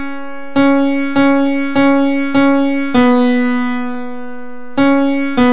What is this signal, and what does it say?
There is no code.